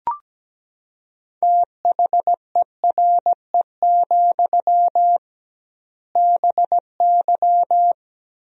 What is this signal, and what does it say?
Morse code, heard as a keyed tone, about 17 wpm, 700 Hz. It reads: THERE, BY
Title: 2x Word-Spacing - Sentences from Top 500 Words - 17wpm